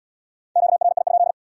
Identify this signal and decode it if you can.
Morse code, heard as a keyed tone, about 70 wpm, 700 Hz. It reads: QRI?